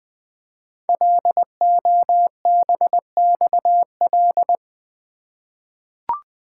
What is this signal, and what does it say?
Morse code, heard as a keyed tone, about 20 wpm, 700 Hz. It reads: LOBXL
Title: Code Groups - 5 Letters - 20wpm